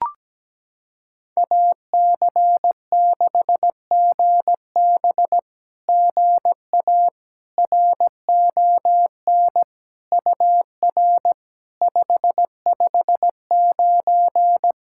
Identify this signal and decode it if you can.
Morse code, heard as a keyed tone, about 18 wpm, 700 Hz. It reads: AC6GB GA RON UR 559
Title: SOTA QSO Practice 17wpm